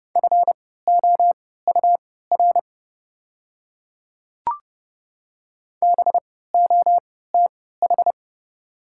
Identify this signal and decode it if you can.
Morse code, heard as a keyed tone, about 30 wpm, spaced as for 17 wpm, 690 Hz. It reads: FOUR BOTH
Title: 3x Character Spacing - Top 500 Words - 30wpm